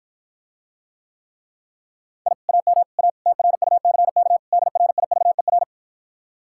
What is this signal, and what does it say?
Morse code, heard as a keyed tone, about 54 wpm, 700 Hz. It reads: I AM A TRUCK DRIVER